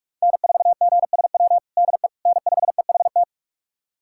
Morse code, heard as a keyed tone, about 46 wpm, 700 Hz. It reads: N4GSW DE N5EHT
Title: USA Callsign DE Callsign - 5 Characters - 45wpm